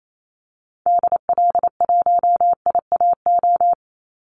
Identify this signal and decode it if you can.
Morse code, heard as a keyed tone, about 28 wpm, 700 Hz. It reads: DL1IAO
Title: Call Signs Repeated 2x - 28wpm